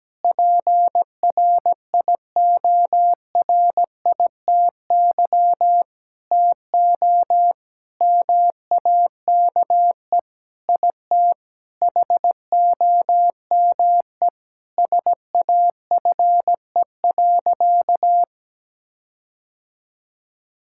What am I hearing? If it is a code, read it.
Morse code, heard as a keyed tone, about 17 wpm, 700 Hz. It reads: PRIORITY TO MAKE IT HOME SAFE.
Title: Sentences from Top 1600 Words 17wpm